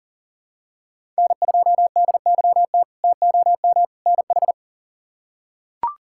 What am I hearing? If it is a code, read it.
Morse code, heard as a keyed tone, about 40 wpm, 700 Hz. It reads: N2DYT TOM NH
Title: K1USN SST QSO Practice - 40wpm